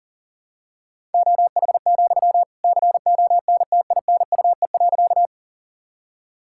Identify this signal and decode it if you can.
Morse code, heard as a keyed tone, about 38 wpm, 700 Hz. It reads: OH, CONTINUE.
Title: Sentences from Top 1000 Words - Review Part 6 - 40wpm